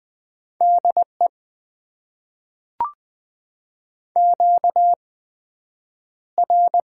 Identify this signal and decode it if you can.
Morse code, heard as a keyed tone, about 20 wpm, spaced as for 9 wpm, 700 Hz. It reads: DE QR
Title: Top 90 QSO Elements - 8x Character Spacing - 20wpm